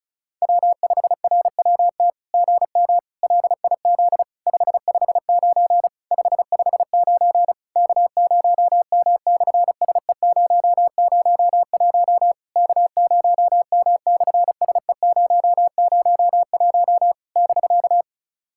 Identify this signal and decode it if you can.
Morse code, heard as a keyed tone, about 36 wpm, 700 Hz. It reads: W5RWT GM LIZ 559 559 K0M/SE001 K0M/SE001 <BK>
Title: SOTA QSO Practice 35wpm